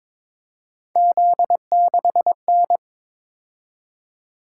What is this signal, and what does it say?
Morse code, heard as a keyed tone, about 23 wpm, 700 Hz. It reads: Z6N
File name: practice / 22wpm / Three Letter-Number Combinations - 22wpm